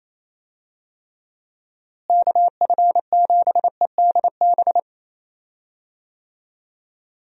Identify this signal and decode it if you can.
Morse code, heard as a keyed tone, about 28 wpm, 700 Hz. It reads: KF7EDB
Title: USA Call Signs - 6 Characters - Repeated 2x - 28wpm